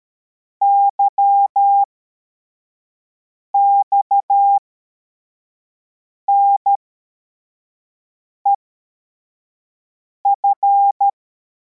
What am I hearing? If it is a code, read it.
Morse code, heard as a keyed tone, about 13 wpm, spaced as for 4 wpm, 800 Hz. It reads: YXNEF